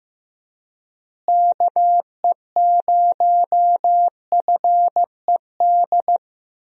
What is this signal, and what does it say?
Morse code, heard as a keyed tone, about 16 wpm, 700 Hz. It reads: KE0FED
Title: USA Call Signs - 6 Characters - Repeated 3x - 15wpm